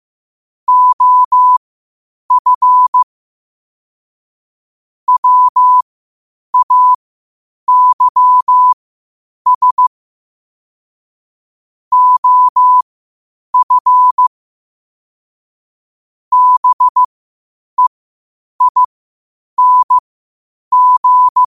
Morse code, heard as a keyed tone, about 15 wpm, spaced as for 8 wpm, 1000 Hz. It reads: OF WAYS OF BEING
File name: qotdfiles / 8wpm